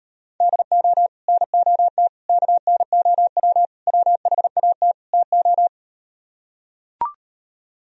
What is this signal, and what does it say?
Morse code, heard as a keyed tone, about 38 wpm, 700 Hz. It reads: DO NOT KNOW WHAT TO